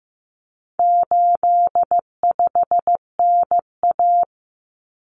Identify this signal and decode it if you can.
Morse code, heard as a keyed tone, about 15 wpm, 700 Hz. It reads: 85NA
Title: Common Cut Numbers - 4-Digit 15wpm